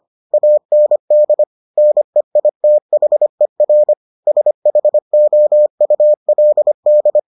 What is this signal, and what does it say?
Morse code, heard as a keyed tone, about 25 wpm, 600 Hz. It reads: AND NEITHER SHOULD